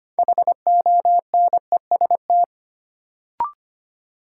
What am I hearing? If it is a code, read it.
Morse code, heard as a keyed tone, about 25 wpm, 700 Hz. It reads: HONEST
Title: Sets of 5 Words Encore - New Word Drill - 25wpm